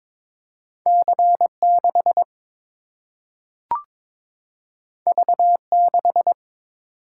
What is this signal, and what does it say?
Morse code, heard as a keyed tone, about 21 wpm, 700 Hz. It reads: C 6 V 6
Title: Single-Letter Single-Number 22wpm